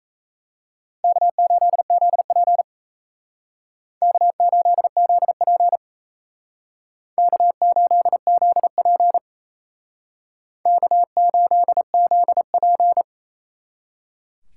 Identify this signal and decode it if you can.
Morse code, heard as a keyed tone, about 34 wpm, 700 Hz. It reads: K8ZP K8ZP K8ZP K8ZP
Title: Speed Racing - USA 1x2 Call Signs 28wpm